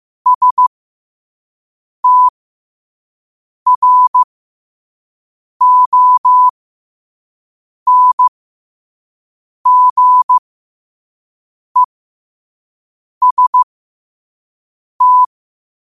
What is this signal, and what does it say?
Morse code, heard as a keyed tone, about 15 wpm, spaced as for 5 wpm, 1000 Hz. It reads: STRONGEST
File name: qotdfiles / 5wpm